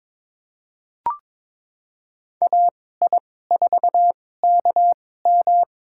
Morse code, heard as a keyed tone, about 22 wpm, spaced as for 16 wpm, 700 Hz. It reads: AI4KM
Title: USA Call Signs - 5 Characters - 2x Character Spacing - 22wpm